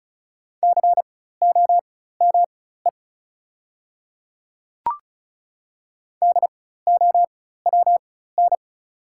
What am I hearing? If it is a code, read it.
Morse code, heard as a keyed tone, about 36 wpm, spaced as for 16 wpm, 690 Hz. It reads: COME DOWN
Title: 4x Character Spacing - Top 100 Words - 35wpm